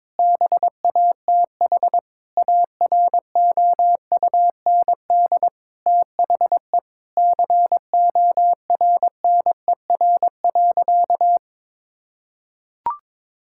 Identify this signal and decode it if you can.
Morse code, heard as a keyed tone, about 22 wpm, 700 Hz. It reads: BATH AROUND THE CORNER.
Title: Sentences from Top 1600 Words 22wpm